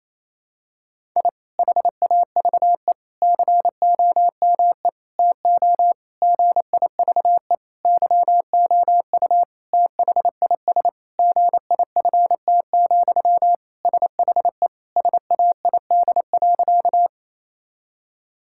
Morse code, heard as a keyed tone, about 28 wpm, 700 Hz. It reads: I HAVE COME TO GIVE YOU THIS GIFT, SHE SAID.
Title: Sentences from Top 1000 Words - Review Part 6 - 28wpm